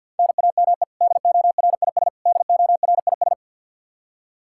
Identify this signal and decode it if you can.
Morse code, heard as a keyed tone, about 52 wpm, 700 Hz. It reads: NAME DORIS DORIS
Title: Practice QSO - Exchange 1 - Beginning - 50wpm